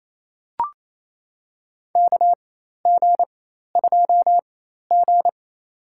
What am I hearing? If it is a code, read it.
Morse code, heard as a keyed tone, about 27 wpm, spaced as for 13 wpm, 700 Hz. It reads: KG2G
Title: USA Call Signs - 4 Characters - 4x Character Spacing - 28wpm